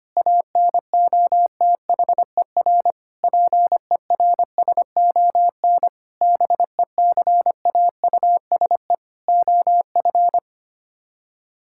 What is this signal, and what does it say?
Morse code, heard as a keyed tone, about 25 wpm, 700 Hz. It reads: ANOTHER PERSON BECAUSE OF
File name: Sets of 9 Words - 25wpm